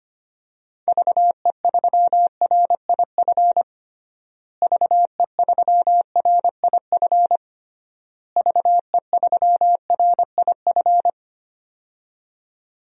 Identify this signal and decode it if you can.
Morse code, heard as a keyed tone, about 25 wpm, 700 Hz. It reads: VE3RIF VE3RIF VE3RIF
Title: DXCC-100 Call Signs - 6 Characters - Repeated 3x - 25wpm